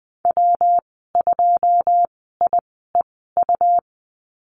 Morse code, heard as a keyed tone, about 20 wpm, spaced as for 15 wpm, 700 Hz. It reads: W2IEU
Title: 154 Lesson 25g - Letter p - Call Signs 20wpm